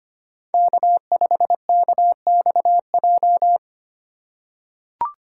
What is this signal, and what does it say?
Morse code, heard as a keyed tone, about 25 wpm, 700 Hz. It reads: K5KXJ
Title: K1USN SST QSO Practice - 25wpm